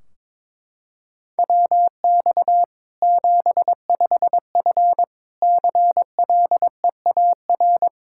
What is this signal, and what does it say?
Morse code, heard as a keyed tone, about 22 wpm, 700 Hz. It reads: WX 75F CLEAR
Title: Practice QSO - Exchange 2 - Info - 22wpm